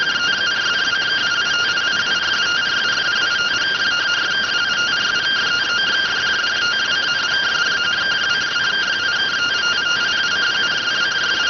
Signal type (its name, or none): two-tone teleprinter signal